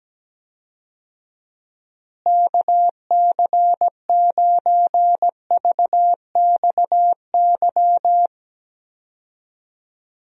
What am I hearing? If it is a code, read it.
Morse code, heard as a keyed tone, about 17 wpm, 700 Hz. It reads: KC9VXY
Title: USA Call Signs - 6 Characters - Repeated 2x - 17wpm